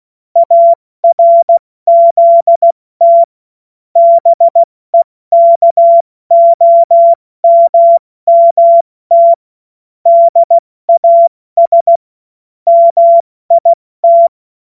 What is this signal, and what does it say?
Morse code, heard as a keyed tone, about 16 wpm, 680 Hz. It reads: ARZT BEKOMMT DAS MIT
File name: Morsekurs / Witze / Witz 12 März 2024 70 F80